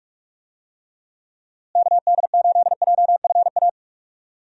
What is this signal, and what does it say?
Morse code, heard as a keyed tone, about 46 wpm, 700 Hz. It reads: KD8JFA
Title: USA Call Signs - 6 Characters - Repeated 3x - 45wpm